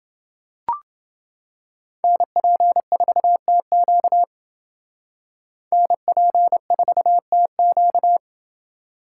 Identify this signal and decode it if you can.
Morse code, heard as a keyed tone, about 28 wpm, 700 Hz. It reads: NP4TQ NP4TQ